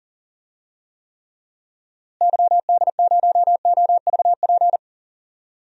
Morse code, heard as a keyed tone, about 37 wpm, 700 Hz. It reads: YD0OVP
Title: DXCC-100 Call Signs - 6 Characters - Repeated 3x - 40wpm